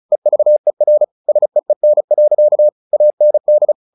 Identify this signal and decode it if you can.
Morse code, heard as a keyed tone, about 34 wpm, 600 Hz. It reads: EVER SEEN. AND
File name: the old man and the sea 2 35wpm